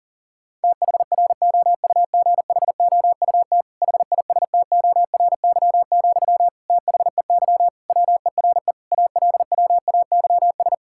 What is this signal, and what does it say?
Morse code, heard as a keyed tone, about 37 wpm, 700 Hz. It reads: THROUGHOUT HISTORY, THEY WERE ALWAYS
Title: Sentences from Top 1000 Words - Review Part 5 - 40wpm